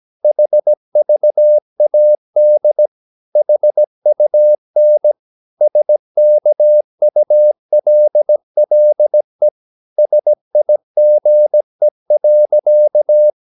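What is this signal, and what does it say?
Morse code, heard as a keyed tone, about 17 wpm, 600 Hz. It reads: HVAD HUN SKULLE SIGE.
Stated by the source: Slow (QRS) Morse practice at 17 wpm, from CW med Gnister.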